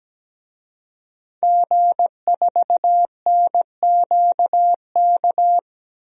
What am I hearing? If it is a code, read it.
Morse code, heard as a keyed tone, about 17 wpm, 700 Hz. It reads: G4NQK
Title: DXCC-100 Call Signs - 5 Characters - 17wpm